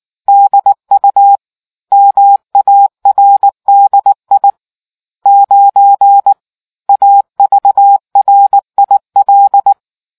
Morse code, heard as a keyed tone, about 19 wpm, 800 Hz. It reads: DU MARDI 9 AVRIL